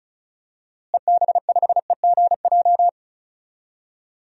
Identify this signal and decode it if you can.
Morse code, heard as a keyed tone, about 36 wpm, 700 Hz. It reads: EB5EGJ
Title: DXCC-100 Call Signs - 6 Characters - Repeated 2x - 35wpm